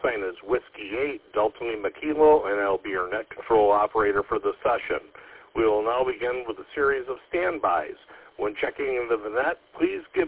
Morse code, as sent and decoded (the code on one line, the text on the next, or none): none